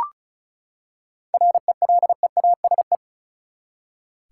Morse code, heard as a keyed tone, about 36 wpm, 700 Hz. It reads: RELEASE
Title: Seven Letter Words - 35wpm